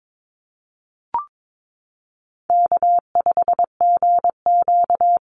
Morse code, heard as a keyed tone, about 22 wpm, 700 Hz. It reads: K5GQ
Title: Call Signs - 22wpm